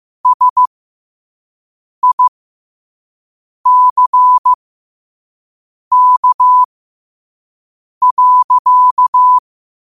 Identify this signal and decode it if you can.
Morse code, heard as a keyed tone, about 15 wpm, spaced as for 5 wpm, 1000 Hz. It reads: SICK.
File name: qotdfiles / 5wpm